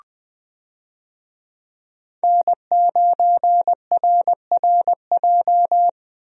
Morse code, heard as a keyed tone, about 20 wpm, 700 Hz. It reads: N9RRJ
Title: USA Call Signs - 5 Characters - 20wpm